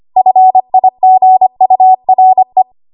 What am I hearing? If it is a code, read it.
Morse code, heard as a keyed tone, about 25 wpm, 750 Hz. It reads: FIGURE
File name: Archive / 25 WPM / 25WPM